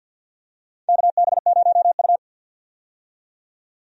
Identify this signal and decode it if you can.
Morse code, heard as a keyed tone, about 52 wpm, 700 Hz. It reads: KB0U
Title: USA Call Signs - 4 Characters - 50wpm